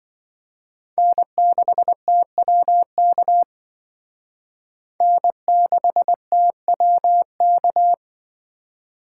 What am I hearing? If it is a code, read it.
Morse code, heard as a keyed tone, about 25 wpm, 700 Hz. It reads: N6TWK N6TWK